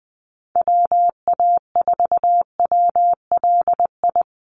Morse code, heard as a keyed tone, about 20 wpm, 700 Hz. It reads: WA4WLI
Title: Call Signs - 20wpm